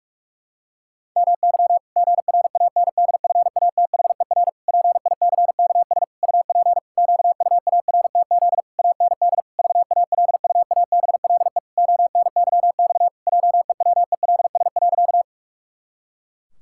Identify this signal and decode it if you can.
Morse code, heard as a keyed tone, about 46 wpm, 700 Hz. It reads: MY GRANDFATHER PICKS UP QUARTZ AND VALUABLE ONYX JEWELS.
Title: Pangram Challenge v2 - 45wpm